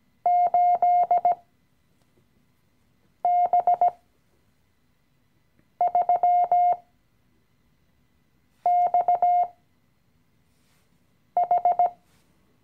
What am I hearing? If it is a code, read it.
Morse code, heard as a keyed tone, about 18 wpm, spaced as for 4 wpm, 710 Hz. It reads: 8B3XH
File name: CW-Online-Kurs / Anfänger / Lektion 016